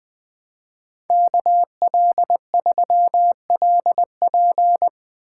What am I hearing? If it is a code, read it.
Morse code, heard as a keyed tone, about 22 wpm, 700 Hz. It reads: KL3LP
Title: Speed Racing - USA 2x2 Call Signs 20wpm